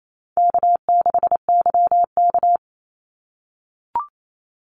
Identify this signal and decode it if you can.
Morse code, heard as a keyed tone, about 28 wpm, 700 Hz. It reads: K6YK
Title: NA SOTA Chaser Callsigns 28wpm